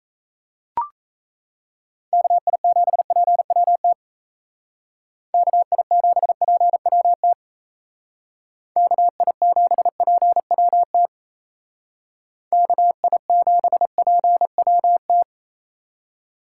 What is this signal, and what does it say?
Morse code, heard as a keyed tone, about 36 wpm, 700 Hz. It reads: KI7PWT KI7PWT KI7PWT KI7PWT